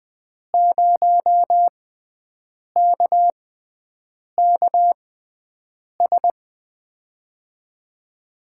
Morse code, heard as a keyed tone, about 20 wpm, spaced as for 7 wpm, 700 Hz. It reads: 0KKS
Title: USA Call Signs - 6 Characters - 6x Character Spacing - 20wpm